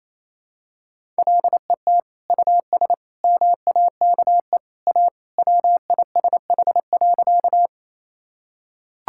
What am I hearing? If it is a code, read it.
Morse code, heard as a keyed tone, about 28 wpm, 700 Hz. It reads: LET US MAKE A WISH.
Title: Sentences from Top 1600 Words 28wpm